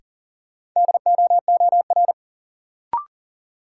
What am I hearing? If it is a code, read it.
Morse code, heard as a keyed tone, about 37 wpm, 700 Hz. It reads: DOOR
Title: Sets of 2 Words - New Word Drill - 40wpm